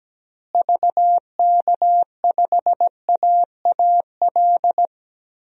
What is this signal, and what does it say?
Morse code, heard as a keyed tone, about 17 wpm, 700 Hz. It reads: VK5AAL
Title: DXCC-100 Call Signs - 6 Characters - Repeated 2x - 17wpm